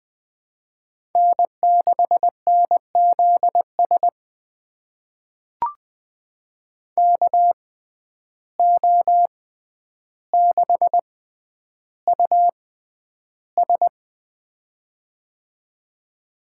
Morse code, heard as a keyed tone, about 21 wpm, spaced as for 10 wpm, 700 Hz. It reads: N6NZS KO6US
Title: USA Call Signs - 5 Characters - 6x Character Spacing - 20wpm